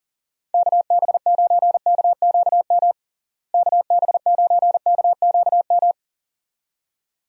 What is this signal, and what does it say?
Morse code, heard as a keyed tone, about 40 wpm, 700 Hz. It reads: KB9KQM KB9KQM
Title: USA Call Signs - 6 Characters - Repeated 2x - 40wpm